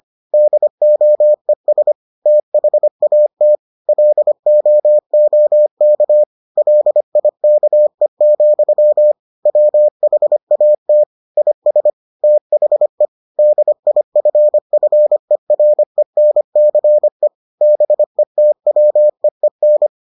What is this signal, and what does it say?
Morse code, heard as a keyed tone, about 25 wpm, 600 Hz. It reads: DOES THAT LOOK LIKE, WHAT IS THE DIFFERENCE BETWEEN